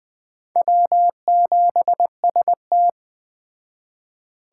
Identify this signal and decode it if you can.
Morse code, heard as a keyed tone, about 20 wpm, 700 Hz. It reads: W7ST
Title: Speed Racing - USA Call Signs 17wpm